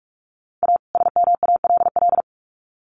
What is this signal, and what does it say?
Morse code, heard as a keyed tone, about 44 wpm, 700 Hz. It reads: A SMALL
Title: Sets of 2 Words 45wpm